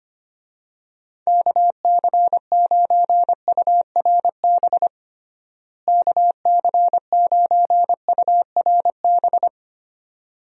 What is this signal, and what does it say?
Morse code, heard as a keyed tone, about 25 wpm, 700 Hz. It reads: KC9URB KC9URB
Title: USA Call Signs - 6 Characters - Repeated 2x - 25wpm